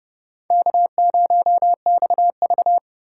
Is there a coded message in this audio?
Yes: Morse code from 0.5 to 2.8 s, heard as a keyed tone, about 29 wpm, 700 Hz. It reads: K0XV